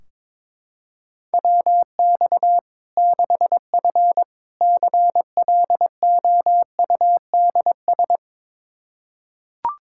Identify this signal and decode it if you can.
Morse code, heard as a keyed tone, about 22 wpm, 700 Hz. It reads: WX 6F CLOUDS
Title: Practice QSO - Exchange 2 - Info - 22wpm